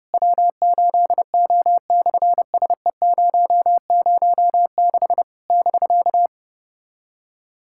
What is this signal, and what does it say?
Morse code, heard as a keyed tone, about 30 wpm, 690 Hz. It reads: W8O/SE006 <BK>
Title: SOTA QSO Practice 30wpm